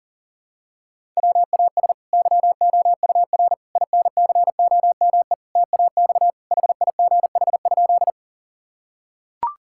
Morse code, heard as a keyed tone, about 40 wpm, 700 Hz. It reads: WAS YOUR INCOME TAX HIGH?
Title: Sentences from Top 1000 Words - Review Part 5 - 40wpm